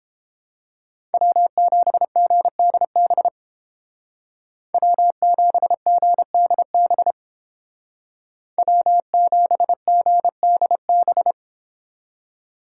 Morse code, heard as a keyed tone, about 31 wpm, 700 Hz. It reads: W7GDB W7GDB W7GDB